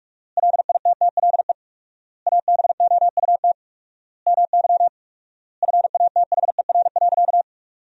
Morse code, heard as a keyed tone, about 46 wpm, 700 Hz. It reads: LITTLE ABOUT MY FATHER.
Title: 4x Word-Spacing - Sentences from Top 400 Words - 45wpm